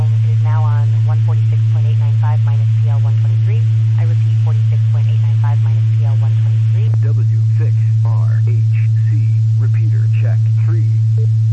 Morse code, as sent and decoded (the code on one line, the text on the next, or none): none